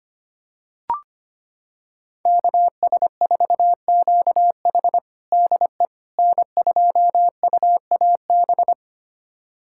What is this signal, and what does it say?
Morse code, heard as a keyed tone, about 25 wpm, 700 Hz. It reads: KS4QH DE N2UAB
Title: USA Callsign DE Callsign - 5 Characters - 25wpm